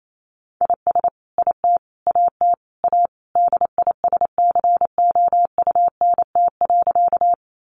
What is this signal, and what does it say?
Morse code, heard as a keyed tone, about 28 wpm, 700 Hz. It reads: IS IT AT A DISCOUNT.
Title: Sentences from Top 1200 Words 28wpm